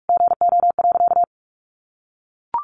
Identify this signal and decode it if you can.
Morse code, heard as a keyed tone, about 44 wpm, 700 Hz. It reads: GO.
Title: Sentences from Top 1400 Words 45wpm